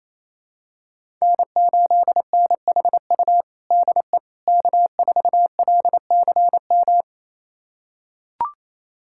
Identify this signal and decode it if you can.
Morse code, heard as a keyed tone, about 28 wpm, 700 Hz. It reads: N8NHU DE K4LCM
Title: USA Callsign DE Callsign - 5 Characters - 28wpm